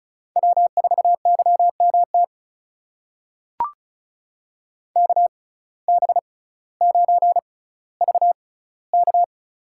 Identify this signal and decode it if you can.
Morse code, heard as a keyed tone, about 36 wpm, spaced as for 18 wpm, 700 Hz. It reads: W4YMT KB9VK
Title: USA Call Signs - 5 Characters - 6x Character Spacing - 35wpm